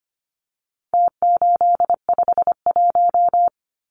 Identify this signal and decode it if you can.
Morse code, heard as a keyed tone, about 25 wpm, 700 Hz. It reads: T851
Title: Numbers - Contest Style - 4-Digit 25wpm